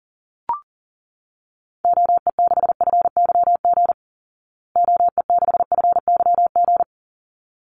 Morse code, heard as a keyed tone, about 40 wpm, 700 Hz. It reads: OE6FYG OE6FYG
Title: Call Signs Repeated 3x - 40wpm